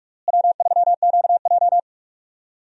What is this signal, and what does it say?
Morse code, heard as a keyed tone, about 46 wpm, 700 Hz. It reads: W3QJ
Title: USA Call Signs - 4 Characters - Repeated 3x - 45wpm